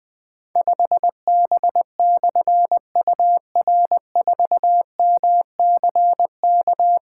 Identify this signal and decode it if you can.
Morse code, heard as a keyed tone, about 20 wpm, 700 Hz. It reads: 5B/UR4MCK
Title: Difficult Call Signs 20wpm